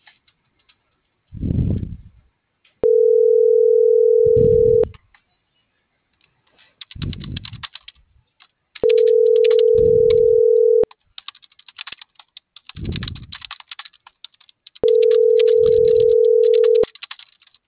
Ringback tone (North America), 440 + 480 Hz.